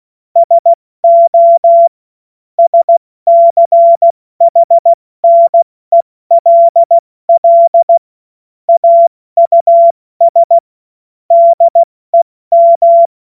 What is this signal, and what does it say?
Morse code, heard as a keyed tone, about 16 wpm, 680 Hz. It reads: SO SCHNELL AUS DEM
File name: Morsekurs / Witze / Witz 20. Juni 2023 70 F80